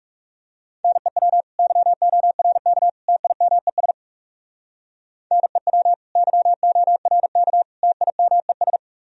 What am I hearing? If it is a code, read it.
Morse code, heard as a keyed tone, about 36 wpm, 700 Hz. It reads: NEW YORK TIMES NEW YORK TIMES
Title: Speed-Racing - Sets of 3 Words 30wpm